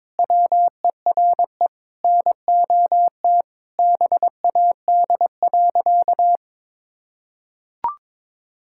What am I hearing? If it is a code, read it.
Morse code, heard as a keyed tone, about 22 wpm, 700 Hz. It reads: WERE NOT BAD.
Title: Sentences from Top 1000 Words - Review Part 5 - 22wpm